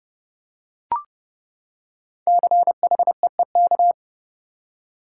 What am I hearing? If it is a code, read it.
Morse code, heard as a keyed tone, about 30 wpm, 700 Hz. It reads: CHEEK
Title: Five Letter Words - 30wpm